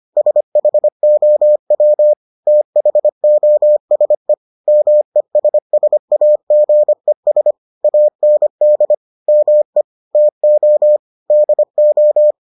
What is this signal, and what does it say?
Morse code, heard as a keyed tone, about 25 wpm, 600 Hz. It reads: SHOW THOSE MESSAGES AND ME TO DO